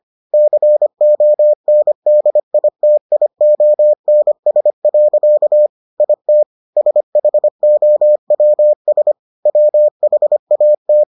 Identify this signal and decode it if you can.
Morse code, heard as a keyed tone, about 25 wpm, 600 Hz. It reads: CONDITIONS. IT SHOWS WHAT